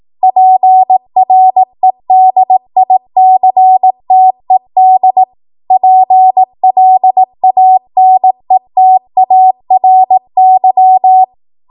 Morse code, heard as a keyed tone, about 18 wpm, 750 Hz. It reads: PREDICTED PLANETARY
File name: Archive / 18 WPM / 18WPM